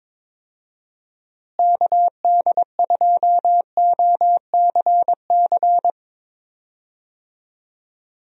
Morse code, heard as a keyed tone, about 21 wpm, 700 Hz. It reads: KD2OCC
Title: USA Call Signs - 6 Characters - Repeated 2x - 22wpm